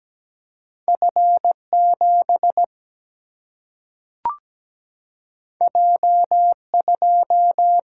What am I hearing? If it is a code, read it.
Morse code, heard as a keyed tone, about 17 wpm, 700 Hz. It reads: F7 J2
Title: Single-Letter Single-Number 17wpm